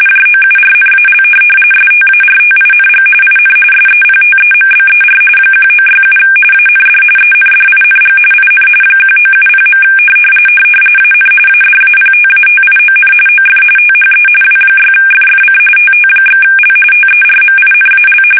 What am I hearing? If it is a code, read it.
RTTY, about 45.45 baud. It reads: JOURNAL STARTING IN SEPTEMBER 1967 AND ENDING IN 1970. AN ADAPTATION OF THE W6FFC TT/L TERMINAL
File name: fsk850 / teletype-RTTY